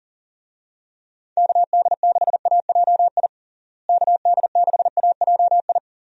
Morse code, heard as a keyed tone, about 40 wpm, 700 Hz. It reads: KD6AJI KD6AJI
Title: USA Call Signs - 6 Characters - Repeated 2x - 40wpm